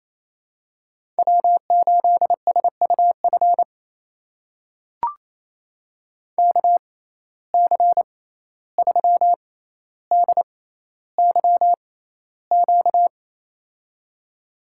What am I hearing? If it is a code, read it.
Morse code, heard as a keyed tone, about 28 wpm, spaced as for 14 wpm, 700 Hz. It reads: W8SUF KC3DYQ
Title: USA Call Signs - 6x Character Spacing - 28wpm